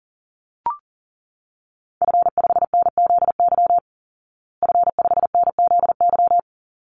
Morse code, heard as a keyed tone, about 40 wpm, 700 Hz. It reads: F5NZY F5NZY
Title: Call Signs Repeated 3x - 40wpm